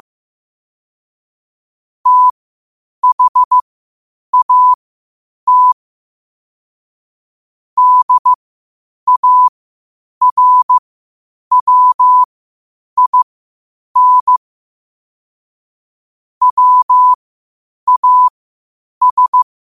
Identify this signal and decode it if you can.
Morse code, heard as a keyed tone, about 15 wpm, spaced as for 8 wpm, 1000 Hz. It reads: THAT DARWIN WAS
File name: qotdfiles / 8wpm